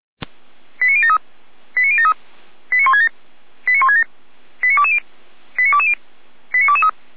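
Five-tone selective call (ZVEI), heard as quick runs of five tones, tones of about 70 ms. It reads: 89073 89073 89167 89167 89220 89220 89223
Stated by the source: ZVEI five-tone call received by SDR.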